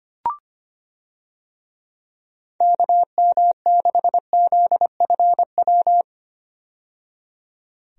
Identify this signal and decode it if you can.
Morse code, heard as a keyed tone, about 25 wpm, 700 Hz. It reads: KM6ZFW
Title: USA Call Signs - 6 Characters - 25wpm